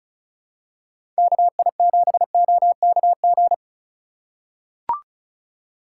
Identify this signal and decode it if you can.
Morse code, heard as a keyed tone, about 36 wpm, 700 Hz. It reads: KI7OKG
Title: USA Call Signs - 6 Characters - 3x Character Spacing - 35wpm